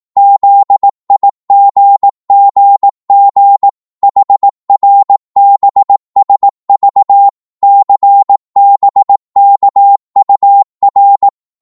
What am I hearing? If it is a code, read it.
Morse code, heard as a keyed tone, about 18 wpm, 800 Hz. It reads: ZIGGG HRBSV CBKUR
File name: Morse / ltr-18wpm